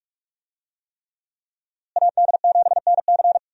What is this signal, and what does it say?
Morse code, heard as a keyed tone, about 46 wpm, 700 Hz. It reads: AD7NC
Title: USA Call Signs - 5 Characters - Repeated 3x - 45wpm